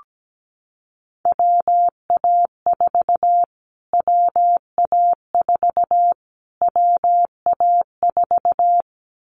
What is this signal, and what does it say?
Morse code, heard as a keyed tone, about 17 wpm, 700 Hz. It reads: WA4 WA4 WA4
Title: Callsign Prefixes for Top 25 DXCC - Repeated 3x - 17wpm